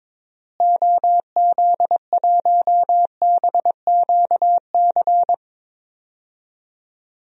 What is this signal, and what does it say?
Morse code, heard as a keyed tone, about 22 wpm, 700 Hz. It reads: OZ1BQC
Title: DXCC-100 Call Signs - 6 Characters - Repeated 2x - 22wpm